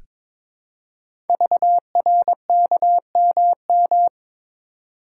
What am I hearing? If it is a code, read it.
Morse code, heard as a keyed tone, about 22 wpm, 700 Hz. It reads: VRKMM
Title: Code Groups - 5 Letters - 22wpm